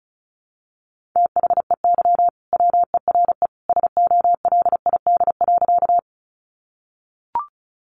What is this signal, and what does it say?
Morse code, heard as a keyed tone, about 34 wpm, 700 Hz. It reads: THEY WERE SOLID.
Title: Sentences from Top 1500 Words 35wpm